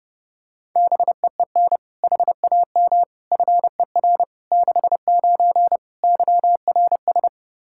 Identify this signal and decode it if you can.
Morse code, heard as a keyed tone, about 30 wpm, 700 Hz. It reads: BEEN HAM FER 69 YRS
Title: Practice QSO - Exchange 3 - More Info - 30wpm